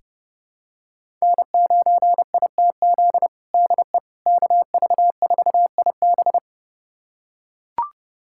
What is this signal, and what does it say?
Morse code, heard as a keyed tone, about 32 wpm, 700 Hz. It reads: N9ITZ DE KV4IB
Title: USA Callsign DE Callsign - 5 Characters - 30wpm